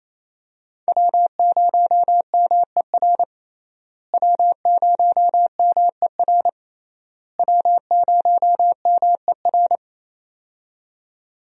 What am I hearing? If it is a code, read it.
Morse code, heard as a keyed tone, about 28 wpm, 700 Hz. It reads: W0MER W0MER W0MER
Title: USA Call Signs - 5 Characters - Repeated 3x - 28wpm